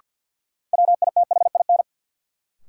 Morse code, heard as a keyed tone, about 53 wpm, 700 Hz. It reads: WITHIN
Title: Sets of 2 Words - New Word Drill - 50wpm